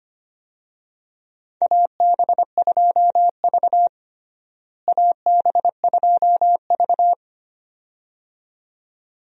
Morse code, heard as a keyed tone, about 25 wpm, 700 Hz. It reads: AB2V AB2V
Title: USA Call Signs - 4 Characters - Repeated 2x - 25wpm